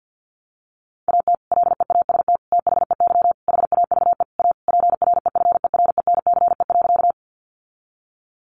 Morse code, heard as a keyed tone, about 50 wpm, 700 Hz. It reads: AT LEAST THEY HAVE A PREFERENCE.
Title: Sentences from Top 1500 Words 50wpm